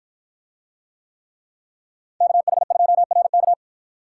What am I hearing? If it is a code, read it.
Morse code, heard as a keyed tone, about 53 wpm, 700 Hz. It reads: KH2RK